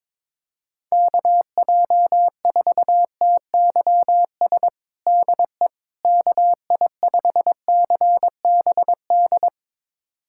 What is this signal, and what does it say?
Morse code, heard as a keyed tone, about 22 wpm, 700 Hz. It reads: KJ4TYS DE KI5CBD
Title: USA Callsign DE Callsign - 6 Characters - 22wpm